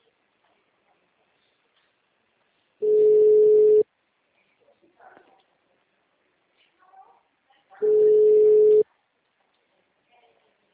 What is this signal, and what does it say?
Ringback tone (Europe, 425 Hz), with speech.
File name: out- -2008- - - .8097